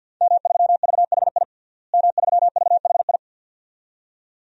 Morse code, heard as a keyed tone, about 51 wpm, 700 Hz. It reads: M3VHI M3VHI
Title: DXCC-100 Call Signs - 5 Characters - Repeated 2x - 50wpm